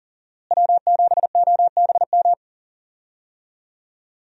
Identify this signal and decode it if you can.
Morse code, heard as a keyed tone, about 40 wpm, 700 Hz. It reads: W7OBM
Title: USA Call Signs - 5 Characters - Repeated 2x - 40wpm